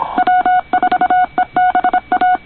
Morse code, heard as a keyed tone, about 25 wpm, 720 Hz. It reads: W4EBA
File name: co8tw / cqwwcw2010